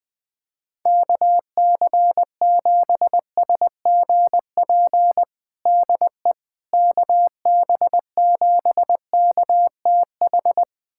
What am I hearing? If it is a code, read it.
Morse code, heard as a keyed tone, about 20 wpm, 700 Hz. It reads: KC7SGP DE KB7KTH
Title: USA Callsign DE Callsign 20wpm